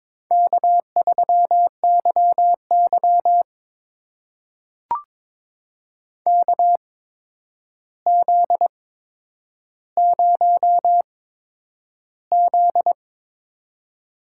Morse code, heard as a keyed tone, about 22 wpm, spaced as for 9 wpm, 700 Hz. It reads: K3YY KZ0Z
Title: USA Call Signs - 4 Characters - 8x Character Spacing - 22wpm